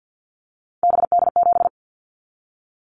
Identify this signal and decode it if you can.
Morse code, heard as a keyed tone, about 50 wpm, 700 Hz. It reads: BD7
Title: Callsign Prefixes for Top 100 DXCC - 50wpm